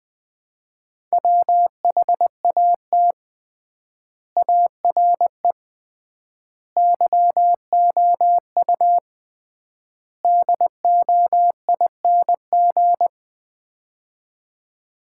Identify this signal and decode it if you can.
Morse code, heard as a keyed tone, about 21 wpm, 700 Hz. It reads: WHAT ARE YOU DOING
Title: 3x Word-Spacing - Sets of 4 Words - 20wpm